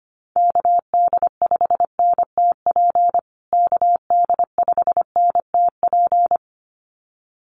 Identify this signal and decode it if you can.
Morse code, heard as a keyed tone, about 25 wpm, 700 Hz. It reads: KD5NTP KD5NTP
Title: USA Call Signs Repeated 2x 25wpm